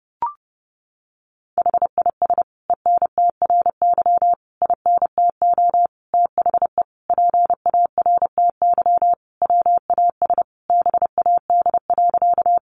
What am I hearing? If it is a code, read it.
Morse code, heard as a keyed tone, about 30 wpm, 700 Hz. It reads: HIS ENTRY INTO THE PARTY WAS BAD.
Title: Sentences from Top 1500 Words 30wpm